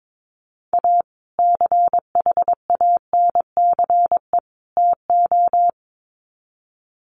Morse code, heard as a keyed tone, about 22 wpm, 700 Hz. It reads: A CHANCE TO
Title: Sets of 3 Words 22wpm